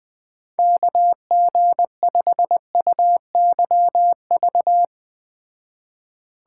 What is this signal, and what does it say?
Morse code, heard as a keyed tone, about 19 wpm, 700 Hz. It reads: KG5UYV
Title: Speed Racing - USA 2x3 Call Signs 17wpm